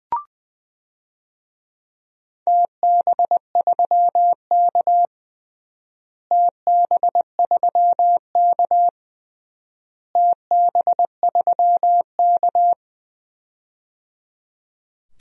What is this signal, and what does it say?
Morse code, heard as a keyed tone, about 21 wpm, 700 Hz. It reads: TB3K TB3K TB3K
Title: DXCC-100 Call Signs - 4 Characters - Repeated 3x - 20wpm